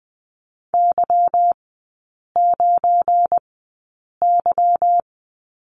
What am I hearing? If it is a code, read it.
Morse code, heard as a keyed tone, about 20 wpm, spaced as for 8 wpm, 690 Hz. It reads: Y9Y
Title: 172 Lesson 28e - Number 9 - Three Characters 20wpm